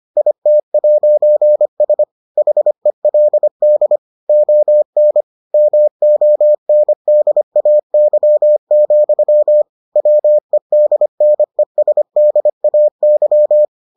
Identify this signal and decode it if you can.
Morse code, heard as a keyed tone, about 25 wpm, 600 Hz. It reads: IT'S HELD ON MONDAY, WEDNESDAY